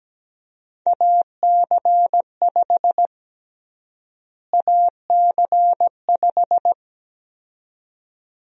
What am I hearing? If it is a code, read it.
Morse code, heard as a keyed tone, about 17 wpm, 700 Hz. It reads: AC5 AC5
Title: USA Call Sign Prefixes - Repeated 2x - 17wpm